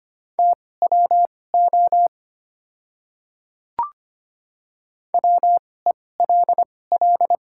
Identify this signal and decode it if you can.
Morse code, heard as a keyed tone, about 25 wpm, spaced as for 18 wpm, 700 Hz. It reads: TWO WELL
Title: 2x Character Spacing - Top 300 Words - 25wpm